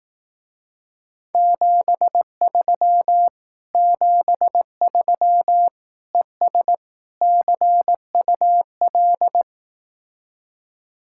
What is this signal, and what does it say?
Morse code, heard as a keyed tone, about 18 wpm, 700 Hz. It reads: 73 73 ES CUL
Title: Speed-Racing Common QSO Phrases Encore - 15wpm